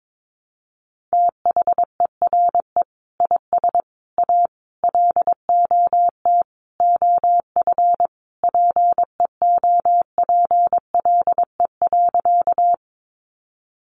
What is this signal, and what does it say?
Morse code, heard as a keyed tone, about 22 wpm, 700 Hz. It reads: THERE IS A LOT OF PEOPLE.
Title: Sentences from Top 500 Words 22wpm